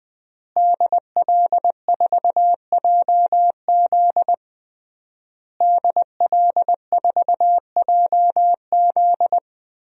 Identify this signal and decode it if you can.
Morse code, heard as a keyed tone, about 21 wpm, 700 Hz. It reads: DL4JZ DL4JZ
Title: DXCC-100 Call Signs - 5 Characters - Repeated 3x - 20wpm